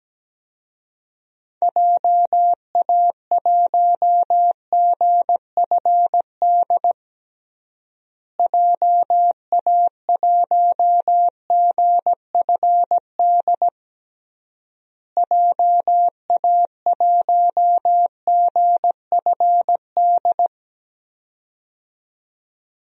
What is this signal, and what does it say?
Morse code, heard as a keyed tone, about 17 wpm, 700 Hz. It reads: JA1GFD JA1GFD JA1GFD
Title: DXCC-100 Call Signs - 6 Characters - Repeated 3x - 17wpm